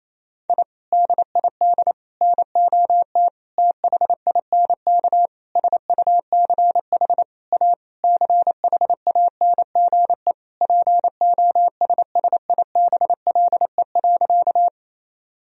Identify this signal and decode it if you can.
Morse code, heard as a keyed tone, about 28 wpm, 700 Hz. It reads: I DID NOT THINK SUCH A CHANGE POSSIBLE.
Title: Sentences from Top 1000 Words - Review Part 1 - 28wpm